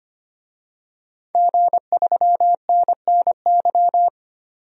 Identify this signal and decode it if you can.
Morse code, heard as a keyed tone, about 25 wpm, 700 Hz. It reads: G3NNY
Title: DXCC-100 Call Signs - 5 Characters - 25wpm